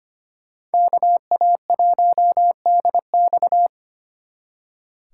Morse code, heard as a keyed tone, about 32 wpm, 700 Hz. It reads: KA1DX